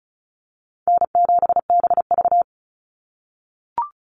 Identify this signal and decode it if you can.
Morse code, heard as a keyed tone, about 34 wpm, 700 Hz. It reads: N7BV
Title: Call Signs - 35wpm